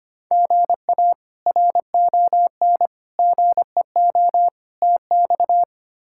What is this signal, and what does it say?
Morse code, heard as a keyed tone, about 25 wpm, 700 Hz. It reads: GA RON GEO TX
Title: K1USN SST QSO Practice - 25wpm